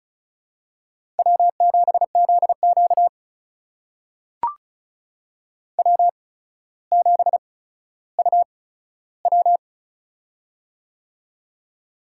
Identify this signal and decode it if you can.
Morse code, heard as a keyed tone, about 36 wpm, spaced as for 15 wpm, 700 Hz. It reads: W7ZQ W7UW
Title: USA Call Signs - 4 Characters - 8x Character Spacing - 35wpm